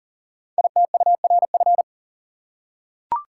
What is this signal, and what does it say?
Morse code, heard as a keyed tone, about 39 wpm, 690 Hz. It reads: ITURF
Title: Code Groups - 5 Letters - 40wpm